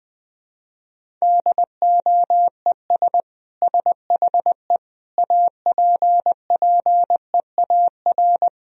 Morse code, heard as a keyed tone, about 21 wpm, 700 Hz. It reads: DOES SHE APPEAR
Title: Sentences from Top 1000 Words - Review Part 2 - 20wpm